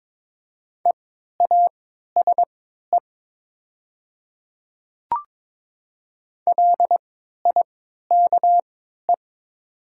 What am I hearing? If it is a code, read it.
Morse code, heard as a keyed tone, about 22 wpm, spaced as for 12 wpm, 690 Hz. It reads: EASE LIKE
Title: 3x Character Spacing - Top 500 Words - 22wpm